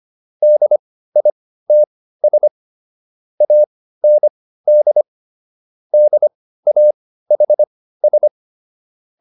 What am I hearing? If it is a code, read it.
Morse code, heard as a keyed tone, about 25 wpm, spaced as for 15 wpm, 600 Hz. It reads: DITS AND DAHS